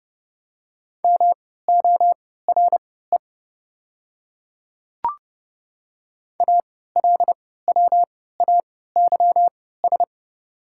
Morse code, heard as a keyed tone, about 30 wpm, spaced as for 17 wpm, 690 Hz. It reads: MORE ALWAYS
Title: 3x Character Spacing - Top 500 Words - 30wpm